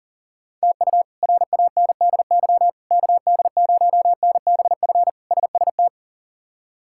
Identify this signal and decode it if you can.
Morse code, heard as a keyed tone, about 40 wpm, 700 Hz. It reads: TU RANDY KD0NBF SST
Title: K1USN SST QSO Practice - 40wpm